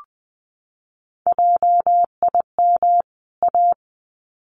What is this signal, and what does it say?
Morse code, heard as a keyed tone, about 20 wpm, 700 Hz. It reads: JIM A
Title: CWT Exchange 20wpm